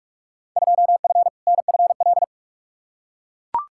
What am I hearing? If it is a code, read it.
Morse code, heard as a keyed tone, about 41 wpm, 700 Hz. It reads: 2F NFL
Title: Field Day Exchanges - 45wpm